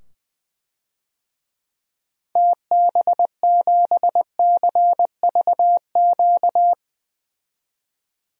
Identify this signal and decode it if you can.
Morse code, heard as a keyed tone, about 20 wpm, 700 Hz. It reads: TB7CVQ
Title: DXCC-100 Call Signs - 6 Characters - 20wpm